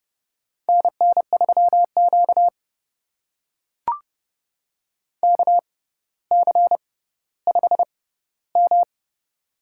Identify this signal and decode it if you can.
Morse code, heard as a keyed tone, about 30 wpm, spaced as for 15 wpm, 700 Hz. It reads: NN3Q KC5M
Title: USA Call Signs - 4 Characters - 6x Character Spacing - 30wpm